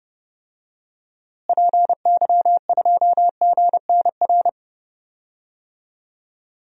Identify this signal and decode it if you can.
Morse code, heard as a keyed tone, about 30 wpm, 700 Hz. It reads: PY2GNR
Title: DXCC-100 Call Signs - 6 Characters - Repeated 2x - 30wpm